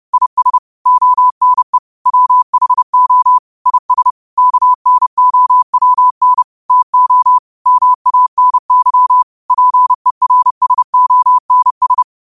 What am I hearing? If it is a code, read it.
Morse code, heard as a keyed tone, about 30 wpm, 1000 Hz. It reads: IS ONE WHO IS KNOWN TO MANY PERSONS
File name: qotdfiles / 30wpm